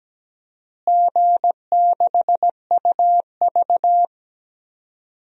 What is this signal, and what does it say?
Morse code, heard as a keyed tone, about 17 wpm, 700 Hz. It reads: G6UV
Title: DXCC-100 Call Signs - 4 Characters - Repeated 3x - 17wpm